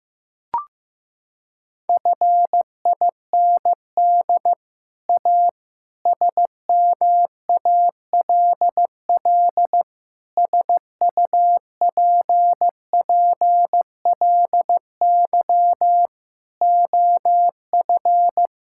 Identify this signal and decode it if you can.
Morse code, heard as a keyed tone, about 15 wpm, 700 Hz. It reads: FIND A SMALL SUPPLY OF
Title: Sentences from Top 1000 Words - Review Part 5 - 15wpm